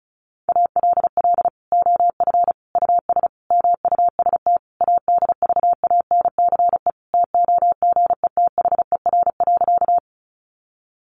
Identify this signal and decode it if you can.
Morse code, heard as a keyed tone, about 34 wpm, 700 Hz. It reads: ALL OF US MUST ADVANCE TOGETHER.
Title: Sentences from Top 1400 Words 35wpm